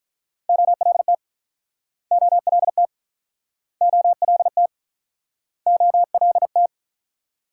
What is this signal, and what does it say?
Morse code, heard as a keyed tone, about 50 wpm, 700 Hz. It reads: OLT OLT OLT OLT